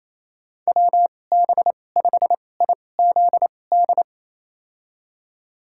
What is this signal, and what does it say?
Morse code, heard as a keyed tone, about 28 wpm, spaced as for 20 wpm, 700 Hz. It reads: WB5IZD
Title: USA Call Signs - 6 Characters - 2x Character Spacing - 28wpm